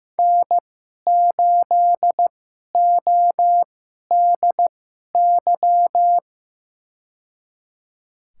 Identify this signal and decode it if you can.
Morse code, heard as a keyed tone, about 15 wpm, spaced as for 11 wpm, 700 Hz. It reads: N8ODY
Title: USA Call Signs - 5 Characters - 2x Character Spacing - 15wpm